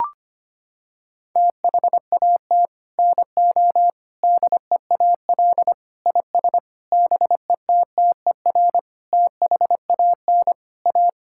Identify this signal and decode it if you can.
Morse code, heard as a keyed tone, about 25 wpm, 700 Hz. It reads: THAT NO DEAL IS BETTER THAN A
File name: Sets of 9 Words - 25wpm